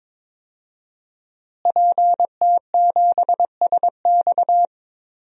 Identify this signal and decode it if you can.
Morse code, heard as a keyed tone, about 22 wpm, 700 Hz. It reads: PT7SX
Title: DXCC-100 Call Signs - 5 Characters - Repeated 2x - 22wpm